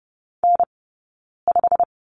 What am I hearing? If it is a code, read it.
Morse code, heard as a keyed tone, about 30 wpm, spaced as for 9 wpm, 690 Hz. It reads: N5
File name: Lesson 17d - Letter u - Two Characters 30wpm